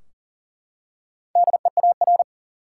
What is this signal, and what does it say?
Morse code, heard as a keyed tone, about 40 wpm, 700 Hz. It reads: DEAR
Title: Four Letter Words - 40wpm